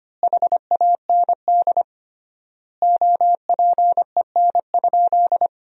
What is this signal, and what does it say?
Morse code, heard as a keyed tone, about 25 wpm, 700 Hz. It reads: HAND OPEN?
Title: 3x Word-Spacing - Sentences from Top 300 Words - 25wpm